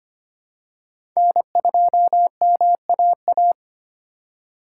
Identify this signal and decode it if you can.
Morse code, heard as a keyed tone, about 25 wpm, 700 Hz. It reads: N2MAA
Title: USA Call Signs - 2x Character Spacing - 25wpm